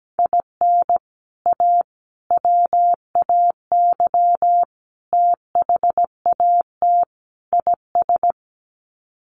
Morse code, heard as a keyed tone, about 17 wpm, 700 Hz. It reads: IN A WAY THAT IS
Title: Sets of 5 Words Encore 17wpm